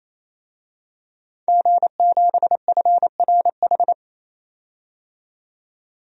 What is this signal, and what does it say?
Morse code, heard as a keyed tone, about 28 wpm, 700 Hz. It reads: G7FRH
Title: DXCC-100 Call Signs - 5 Characters - 28wpm